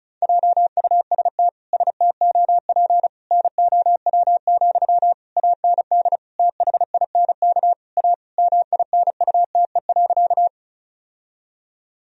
Morse code, heard as a keyed tone, about 36 wpm, 700 Hz. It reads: JUST STOP NOW, AND THINK A MINUTE.
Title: Sentences from Top 1000 Words - Review Part 1 - 35wpm